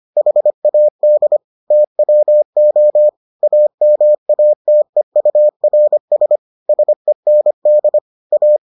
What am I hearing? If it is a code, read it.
Morse code, heard as a keyed tone, about 25 wpm, 600 Hz. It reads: HAD TWO AMATEURS SEND A